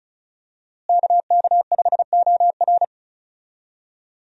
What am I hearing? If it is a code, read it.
Morse code, heard as a keyed tone, about 36 wpm, 700 Hz. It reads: KK5OR
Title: USA Call Signs - 5 Characters - Repeated 3x - 35wpm